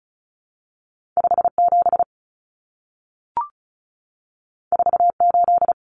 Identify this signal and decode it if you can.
Morse code, heard as a keyed tone, about 36 wpm, 700 Hz. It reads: 57 48
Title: Common Cut Numbers - 2-Digit 35wpm